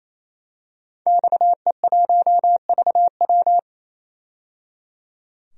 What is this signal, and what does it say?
Morse code, heard as a keyed tone, about 28 wpm, 700 Hz. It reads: XE1VW
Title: DXCC-100 Call Signs - 5 Characters - 28wpm